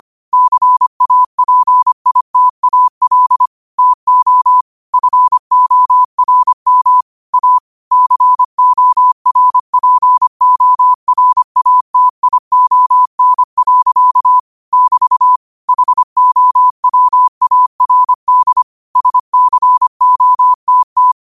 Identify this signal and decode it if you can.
Morse code, heard as a keyed tone, about 25 wpm, 1000 Hz. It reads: CAPITAL TO FORM A CORPORATION. <BT> HOWARD SCOTT
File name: qotdfiles / 25wpm